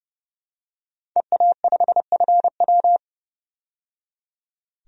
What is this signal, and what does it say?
Morse code, heard as a keyed tone, about 30 wpm, 700 Hz. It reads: EA5FW
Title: DXCC-100 Call Signs - 5 Characters - 30wpm